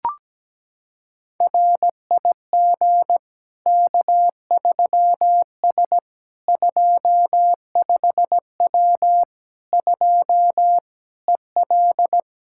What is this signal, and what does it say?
Morse code, heard as a keyed tone, about 17 wpm, 700 Hz. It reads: RIG K3S 25W 2 EL
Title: Practice QSO - Exchange 2 - Info - 17wpm